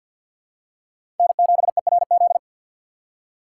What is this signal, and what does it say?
Morse code, heard as a keyed tone, about 51 wpm, 700 Hz. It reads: N7ERZ
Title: USA Call Signs - 5 Characters - 2x Character Spacing - 50wpm